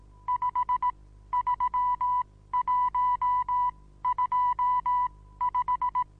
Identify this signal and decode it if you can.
Morse code, heard as a keyed tone, about 18 wpm, spaced as for 14 wpm, 990 Hz. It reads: 53125
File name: Onlinelehrgang / morsen / DJ4UF Morsen25